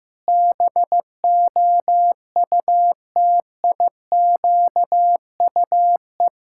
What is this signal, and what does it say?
Morse code, heard as a keyed tone, about 15 wpm, 700 Hz. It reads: BOUTIQUE
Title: Pangram Challenge v2 - 15wpm